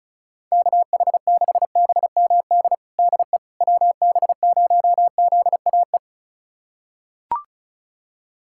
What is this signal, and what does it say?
Morse code, heard as a keyed tone, about 36 wpm, 700 Hz. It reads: KH6BMD DE WB0ZAE
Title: USA Callsign DE Callsign - 6 Characters - 35wpm